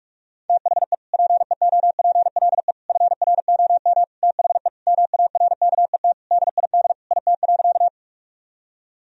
Morse code, heard as a keyed tone, about 42 wpm, 700 Hz. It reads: THE PEOPLE FROM THE MARKET DID IT.
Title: Sentences from Top 1000 Words - Review Part 3 - 45wpm